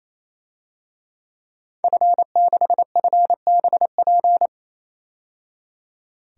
Morse code, heard as a keyed tone, about 28 wpm, 700 Hz. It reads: F6FBP
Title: DXCC-100 Call Signs - 5 Characters - 28wpm